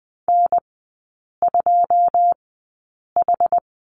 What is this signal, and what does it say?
Morse code, heard as a keyed tone, about 20 wpm, spaced as for 8 wpm, 700 Hz. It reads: N2H
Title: 081 Lesson 14e - Number 2 - Three Characters 20wpm